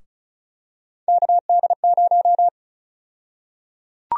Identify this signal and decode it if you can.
Morse code, heard as a keyed tone, about 36 wpm, 700 Hz. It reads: KD0
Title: USA Call Sign Prefixes - Repeated 2x - 35wpm